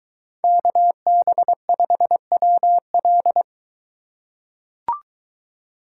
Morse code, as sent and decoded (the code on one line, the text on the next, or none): -.- -... ..... .-- .-..
KB5WL